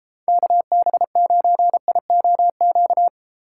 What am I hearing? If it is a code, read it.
Morse code, heard as a keyed tone, about 33 wpm, 700 Hz. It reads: KB9IOQ